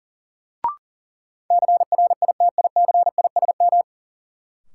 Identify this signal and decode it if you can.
Morse code, heard as a keyed tone, about 40 wpm, 700 Hz. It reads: CRITICISM
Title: Sentences from Top 1300 Words - New Word Drill - 40wpm